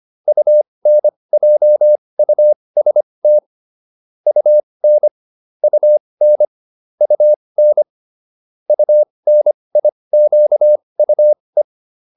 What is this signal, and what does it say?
Morse code, heard as a keyed tone, about 25 wpm, spaced as for 20 wpm, 600 Hz. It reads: UNJUST UN UN UN UNIQUE